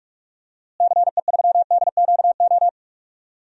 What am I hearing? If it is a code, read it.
Morse code, heard as a keyed tone, about 44 wpm, 700 Hz. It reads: CE3DQO
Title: DXCC-100 Call Signs - 6 Characters - Repeated 2x - 45wpm